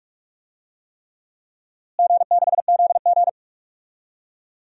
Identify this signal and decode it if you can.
Morse code, heard as a keyed tone, about 46 wpm, 700 Hz. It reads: G6ZG
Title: DXCC-100 Call Signs - 4 Characters - Repeated 3x - 45wpm